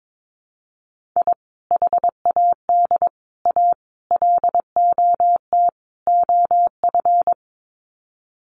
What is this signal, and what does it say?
Morse code, heard as a keyed tone, about 22 wpm, 700 Hz. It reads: I HAD A LOT OF
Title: Sets of 5 Words Encore 22wpm